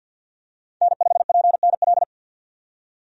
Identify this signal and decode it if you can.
Morse code, heard as a keyed tone, about 49 wpm, 700 Hz. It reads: N5PNL
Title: USA Call Signs - 5 Characters - 50wpm